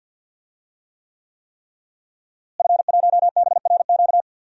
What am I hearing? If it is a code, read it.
Morse code, heard as a keyed tone, about 50 wpm, 700 Hz. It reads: F1BRQ
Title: DXCC-100 Call Signs - 5 Characters - Repeated 3x - 50wpm